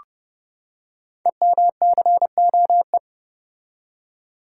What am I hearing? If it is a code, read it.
Morse code, heard as a keyed tone, about 30 wpm, 700 Hz. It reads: EMCOE
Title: Code Groups - 5 Letter:Numbers - 30wpm